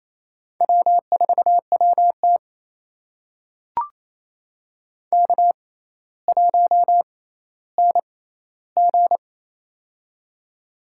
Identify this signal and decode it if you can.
Morse code, heard as a keyed tone, about 28 wpm, spaced as for 14 wpm, 700 Hz. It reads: W4WT K1NG
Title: USA Call Signs - 4 Characters - 6x Character Spacing - 28wpm